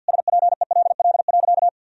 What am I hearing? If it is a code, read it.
Morse code, heard as a keyed tone, about 50 wpm, 700 Hz. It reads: SPELL.
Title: Sentences from Top 1000 Words - Review Part 1 - 50wpm